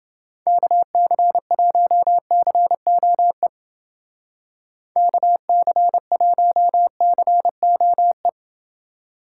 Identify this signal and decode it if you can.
Morse code, heard as a keyed tone, about 29 wpm, 700 Hz. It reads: KC1COE KC1COE